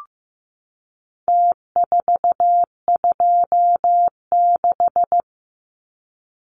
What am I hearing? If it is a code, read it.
Morse code, heard as a keyed tone, about 15 wpm, 700 Hz. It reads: T426
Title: Numbers - Contest Style - 4-Digit 15wpm